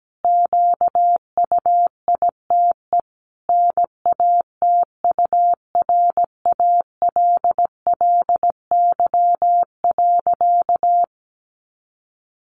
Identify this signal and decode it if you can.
Morse code, heard as a keyed tone, about 17 wpm, 700 Hz. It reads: QUITE NATURALLY.
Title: Sentences from Top 1300 Words 17wpm